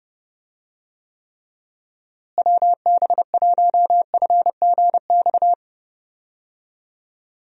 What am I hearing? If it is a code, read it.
Morse code, heard as a keyed tone, about 32 wpm, 700 Hz. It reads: WB1FGX
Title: USA Call Signs - 6 Characters - Repeated 3x - 30wpm